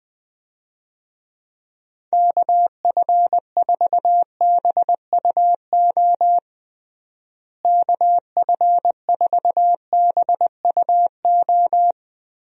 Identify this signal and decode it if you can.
Morse code, heard as a keyed tone, about 20 wpm, 700 Hz. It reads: KF4BUO KF4BUO
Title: USA Call Signs - 6 Characters - Repeated 3x - 20wpm